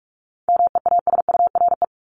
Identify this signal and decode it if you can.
Morse code, heard as a keyed tone, about 44 wpm, 700 Hz. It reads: MEASURE
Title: Top 500 Words 45wpm